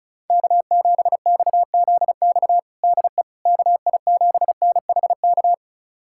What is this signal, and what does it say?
Morse code, heard as a keyed tone, about 34 wpm, 700 Hz. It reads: K7XZX DE KI7NHK
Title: USA Callsign DE Callsign 35wpm